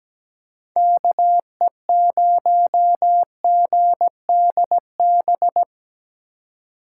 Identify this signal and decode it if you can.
Morse code, heard as a keyed tone, about 17 wpm, 700 Hz. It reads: KE0GDB
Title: USA Call Signs - 6 Characters - 3x Character Spacing - 17wpm